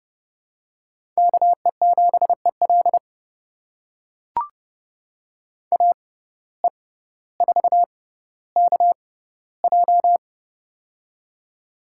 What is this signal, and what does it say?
Morse code, heard as a keyed tone, about 30 wpm, spaced as for 15 wpm, 700 Hz. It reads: KE7EL AE4KJ
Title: USA Call Signs - 5 Characters - 6x Character Spacing - 30wpm